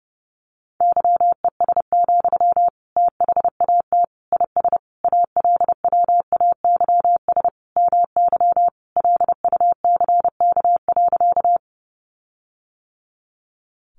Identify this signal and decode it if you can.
Morse code, heard as a keyed tone, about 30 wpm, 700 Hz. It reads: YES, THAT IS ALWAYS MY LUCK.
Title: Sentences from Top 1500 Words 30wpm